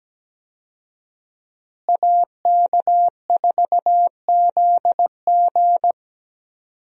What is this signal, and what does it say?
Morse code, heard as a keyed tone, about 17 wpm, 700 Hz. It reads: AK4ZG
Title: USA Call Signs - 5 Characters - Repeated 2x - 17wpm